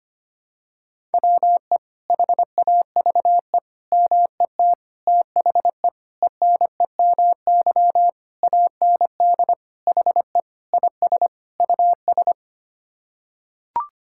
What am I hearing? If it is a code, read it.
Morse code, heard as a keyed tone, about 25 wpm, 700 Hz. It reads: WE HAVE MET THE ENEMY AND HE IS US
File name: Sets of 9 Words - 25wpm